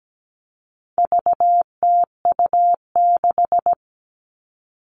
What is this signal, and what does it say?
Morse code, heard as a keyed tone, about 17 wpm, 700 Hz. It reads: VTU6
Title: All Cut Numbers - 4-Digit 17wpm